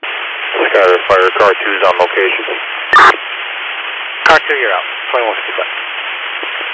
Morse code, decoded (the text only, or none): none